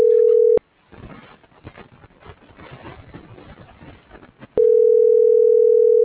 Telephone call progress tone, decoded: ringback tone

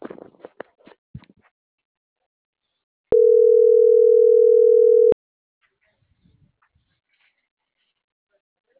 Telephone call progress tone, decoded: ringback tone

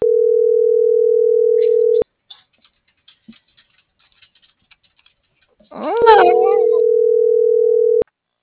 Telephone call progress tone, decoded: ringback tone

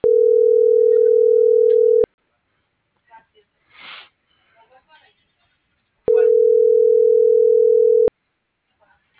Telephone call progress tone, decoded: ringback tone

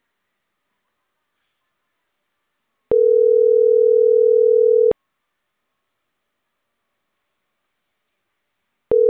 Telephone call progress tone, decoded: ringback tone

